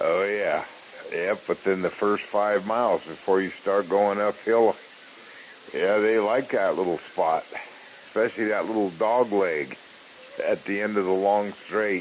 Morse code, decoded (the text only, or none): none